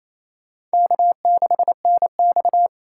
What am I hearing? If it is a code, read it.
Morse code, heard as a keyed tone, about 34 wpm, 700 Hz. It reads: K6NX
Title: Speed Racing - USA 1x2 Call Signs 28wpm